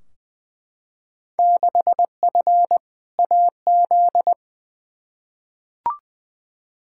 Morse code, heard as a keyed tone, about 20 wpm, 700 Hz. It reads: 6F AZ
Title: Field Day Exchanges Encore - 20wpm